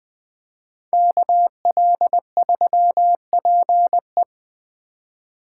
Morse code, heard as a keyed tone, about 21 wpm, 700 Hz. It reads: KL3PE
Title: Speed Racing - USA 2x2 Call Signs 17wpm